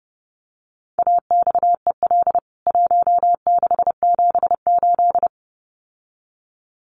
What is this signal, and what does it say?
Morse code, heard as a keyed tone, about 30 wpm, 700 Hz. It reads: AXEL 1678
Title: CWT Exchange 30wpm